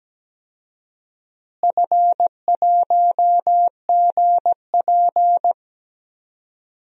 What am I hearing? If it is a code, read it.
Morse code, heard as a keyed tone, about 17 wpm, 700 Hz. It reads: F1GP
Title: DXCC-100 Call Signs - 4 Characters - Repeated 3x - 17wpm